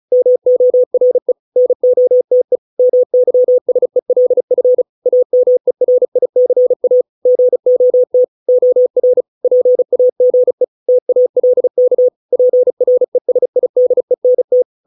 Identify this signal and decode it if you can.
Morse code, heard as a keyed tone, about 34 wpm, 500 Hz. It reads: MORE NOTE MYSELF AMERICA GOT OR PAGE TALK PRESIDENT